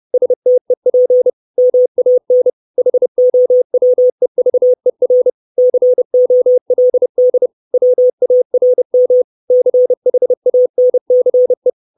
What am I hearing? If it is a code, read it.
Morse code, heard as a keyed tone, about 30 wpm, 500 Hz. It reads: STEP MAN HOWEVER COLD WARM CHANCE
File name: 632 Words 30WPM